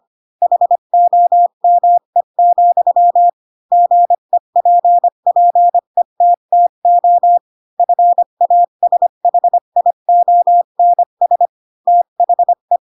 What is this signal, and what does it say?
Morse code, heard as a keyed tone, about 25 wpm, 700 Hz. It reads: HOME, GEPPETTO FASHIONS THE